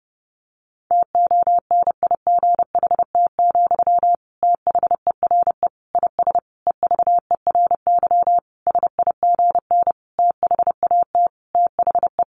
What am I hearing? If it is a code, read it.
Morse code, heard as a keyed tone, about 30 wpm, 700 Hz. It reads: TONIGHT, THERE IS EVERY SIGN THAT THE
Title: Sentences from Top 1400 Words 30wpm